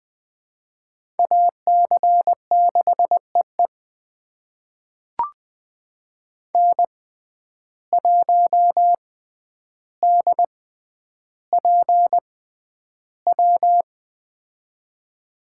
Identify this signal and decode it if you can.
Morse code, heard as a keyed tone, about 20 wpm, spaced as for 10 wpm, 700 Hz. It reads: AC6EE N1DPW
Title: USA Call Signs - 5 Characters - 6x Character Spacing - 20wpm